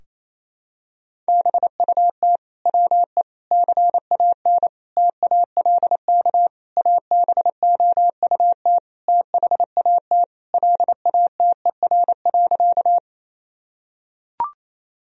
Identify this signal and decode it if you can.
Morse code, heard as a keyed tone, about 27 wpm, 700 Hz. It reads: BUT WE CAN TALK ABOUT THAT LATER.
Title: Sentences from Top 1000 Words - Review Part 4 - 28wpm